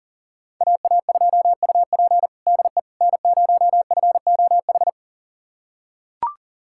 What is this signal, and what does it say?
Morse code, heard as a keyed tone, about 37 wpm, 700 Hz. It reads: AA2UP DE N0FOH
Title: USA Callsign DE Callsign - 5 Characters - 40wpm